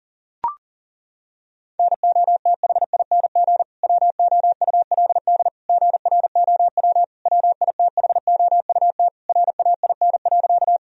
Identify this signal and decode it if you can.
Morse code, heard as a keyed tone, about 40 wpm, 700 Hz. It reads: NOTHING WOULD GROW WITHOUT RAIN.
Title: Sentences from Top 1000 Words - Review Part 3 - 40wpm